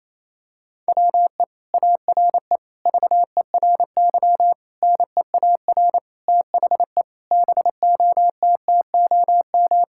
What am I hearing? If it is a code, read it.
Morse code, heard as a keyed tone, about 28 wpm, 700 Hz. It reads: WE ARE VERY NEAR THE BOTTOM
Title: Sentences from Top 1000 Words - Review Part 6 - 28wpm